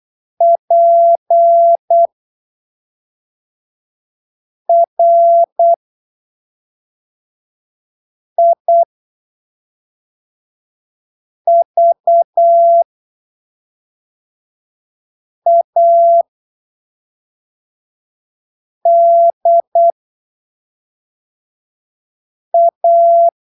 Morse code, heard as a keyed tone, about 8 wpm, spaced as for 3 wpm, 680 Hz. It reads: PRIVADA
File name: arquivos / texto / texto 28 08ppm